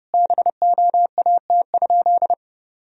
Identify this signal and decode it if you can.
Morse code, heard as a keyed tone, about 30 wpm, 700 Hz. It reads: BOAT?
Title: Sentences from Top 1000 Words - Review Part 3 - 30wpm